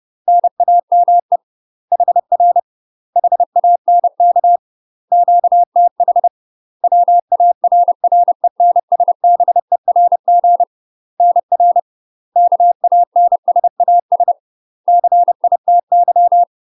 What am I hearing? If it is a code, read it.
Morse code, heard as a keyed tone, about 30 wpm, 700 Hz. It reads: NAME HR HANK QTH WARRENSBERG NR KANSAS CITY
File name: UK MASTER QSOs TRAINING / 30WPM / UK MASTER QSOs 0016